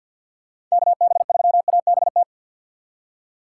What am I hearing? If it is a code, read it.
Morse code, heard as a keyed tone, about 52 wpm, 700 Hz. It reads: KB3ABT
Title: USA Call Signs - 6 Characters - 4x Character Spacing - 50wpm